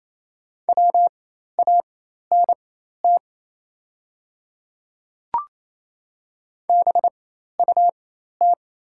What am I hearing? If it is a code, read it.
Morse code, heard as a keyed tone, about 28 wpm, spaced as for 13 wpm, 700 Hz. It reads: WANT BUT
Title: 4x Character Spacing - Top 500 Words - 28wpm